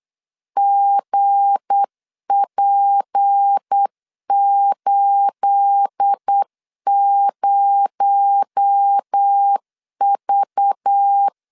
Morse code, heard as a keyed tone, about 8 wpm, 800 Hz. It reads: GP80V
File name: LEZIONI / 9 28 8wpm 3512